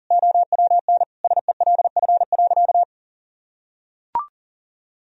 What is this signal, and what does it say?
Morse code, heard as a keyed tone, about 37 wpm, 700 Hz. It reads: OWN SELF.
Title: Sentences from Top 1000 Words - Review Part 1 - 40wpm